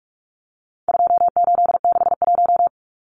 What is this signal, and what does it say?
Morse code, heard as a keyed tone, about 44 wpm, 700 Hz. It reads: 2861
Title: Numbers - 4-Digit 45wpm